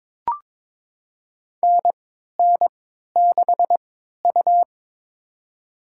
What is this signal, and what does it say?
Morse code, heard as a keyed tone, about 22 wpm, spaced as for 13 wpm, 700 Hz. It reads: NN6U
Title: USA Call Signs - 4 Characters - 3x Character Spacing - 22wpm